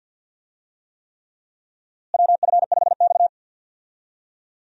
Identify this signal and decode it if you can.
Morse code, heard as a keyed tone, about 49 wpm, 700 Hz. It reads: WF5X
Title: USA Call Signs - 4 Characters - Repeated 3x - 50wpm